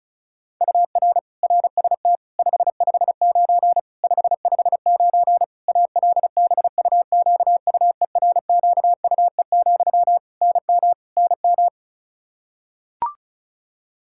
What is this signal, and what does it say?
Morse code, heard as a keyed tone, about 34 wpm, 700 Hz. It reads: UR RST 559 559 ALBUQUERQUE, NM NM
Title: Practice QSO - Exchange 1 - Beginning - 35wpm